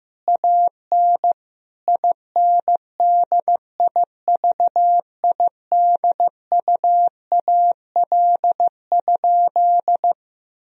Morse code, heard as a keyed tone, about 15 wpm, 700 Hz. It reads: AN INDIVIDUAL?
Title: Sentences from Top 1000 Words - Review Part 4 - 15wpm